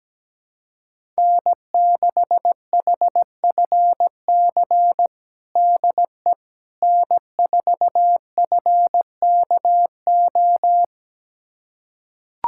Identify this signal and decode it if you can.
Morse code, heard as a keyed tone, about 17 wpm, 700 Hz. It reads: N6HFC DE N4FKO
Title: USA Callsign DE Callsign - 5 Characters - 17wpm